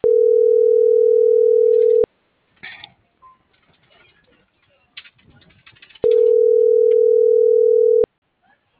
Ringback tone (North America), 440 + 480 Hz.